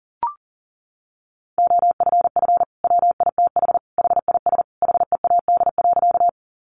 Morse code, heard as a keyed tone, about 40 wpm, 700 Hz. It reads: OFF WITH HIS HEAD.
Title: Sentences from Top 300 Words 40wpm